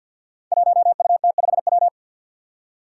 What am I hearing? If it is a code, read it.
Morse code, heard as a keyed tone, about 45 wpm, 700 Hz. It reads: 1UT5W
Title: Five Challenging Letter-Number Combinations 50wpm